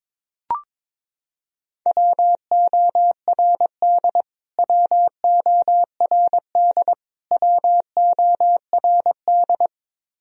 Morse code, heard as a keyed tone, about 22 wpm, 700 Hz. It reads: WORD WORD WORD
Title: Top 100 Words Repeated 22wpm